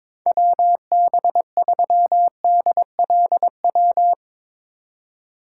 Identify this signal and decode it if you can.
Morse code, heard as a keyed tone, about 22 wpm, 700 Hz. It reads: WB3DLW
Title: USA Call Signs - 6 Characters - 4x Character Spacing - 22wpm